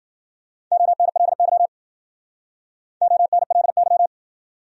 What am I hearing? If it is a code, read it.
Morse code, heard as a keyed tone, about 54 wpm, 700 Hz. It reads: ONLY ONLY